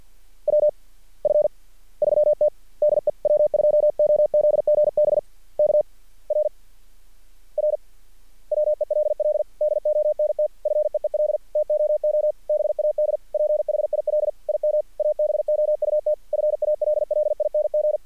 Morse code, heard as a keyed tone, about 46 wpm, spaced as for 37 wpm, 600 Hz. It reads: WF3T DE W2OZGB K RR WELL DONT FEEL TOO BAD PHIL IM ABOUT FALLING
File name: archived / CRSnet